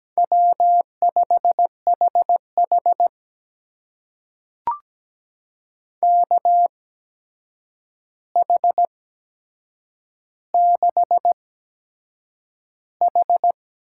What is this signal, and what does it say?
Morse code, heard as a keyed tone, about 17 wpm, spaced as for 7 wpm, 700 Hz. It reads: W5HH KH6H
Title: USA Call Signs - 4 Characters - 8x Character Spacing - 17wpm